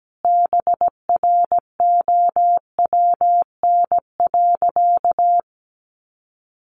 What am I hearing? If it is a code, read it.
Morse code, heard as a keyed tone, about 17 wpm, 700 Hz. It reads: BROWN.
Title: Sentences from Top 1200 Words 17wpm